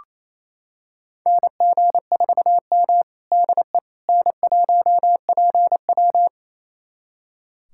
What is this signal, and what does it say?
Morse code, heard as a keyed tone, about 28 wpm, 700 Hz. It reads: NG4M DE N1PW
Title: USA Callsign DE Callsign - 4 Characters - 28wpm